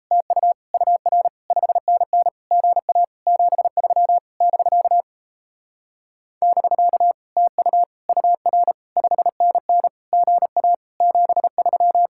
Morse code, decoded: TU UR 5NN GA 73 <BK> <BK> TU UR 5NN GA 73

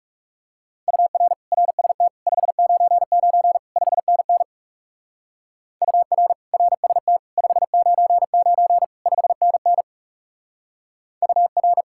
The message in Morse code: UR RST 599 5NN UR RST 599 5NN UR

